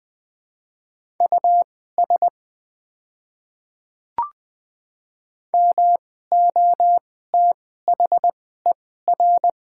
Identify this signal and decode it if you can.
Morse code, heard as a keyed tone, about 21 wpm, spaced as for 15 wpm, 690 Hz. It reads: US MOTHER
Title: 2x Character Spacing - Top 300 Words - 20wpm